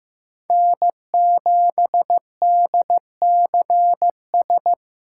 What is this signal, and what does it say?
Morse code, heard as a keyed tone, about 15 wpm, 700 Hz. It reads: N7DCS
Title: USA Callsign DE Callsign 15wpm